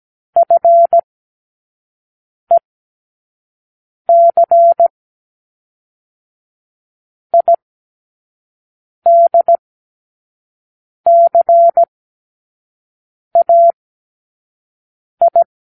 Morse code, heard as a keyed tone, about 17 wpm, 680 Hz. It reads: F E C I D C A I
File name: k7qo / Lesson15(A-J)